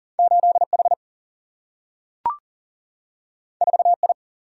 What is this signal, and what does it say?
Morse code, heard as a keyed tone, about 40 wpm, 700 Hz. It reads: 8H 4I
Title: Single-Letter Single-Number 40wpm